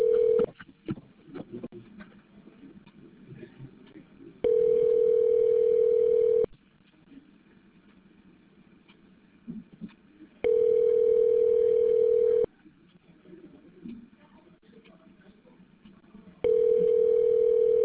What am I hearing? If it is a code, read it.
Ringback tone (North America), 440 + 480 Hz.